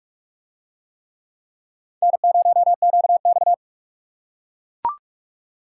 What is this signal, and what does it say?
Morse code, heard as a keyed tone, about 44 wpm, 700 Hz. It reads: N0QX